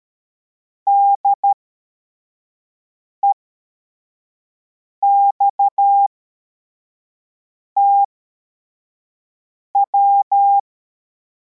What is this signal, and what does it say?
Morse code, heard as a keyed tone, about 13 wpm, spaced as for 4 wpm, 800 Hz. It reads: DEXTW
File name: formation / cw / Text71